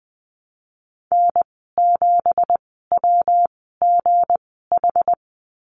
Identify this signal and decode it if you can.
Morse code, heard as a keyed tone, about 20 wpm, spaced as for 14 wpm, 700 Hz. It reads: N7WGH
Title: 193 Lesson 31g - Letter v - Call Signs 20wpm